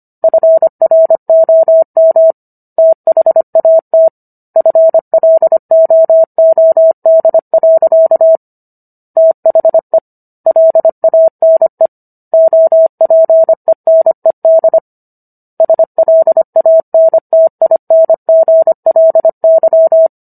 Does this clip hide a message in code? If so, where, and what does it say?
Yes: Morse code from 0.2 to 20.1 s, heard as a keyed tone, about 25 wpm, 650 Hz. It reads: FROM THAT FLOOD. THE LANE OPENED SLANTINGLY